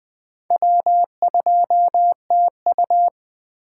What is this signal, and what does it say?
Morse code, heard as a keyed tone, about 21 wpm, 700 Hz. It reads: W2TU
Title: USA Call Signs - 8x Character Spacing - 20wpm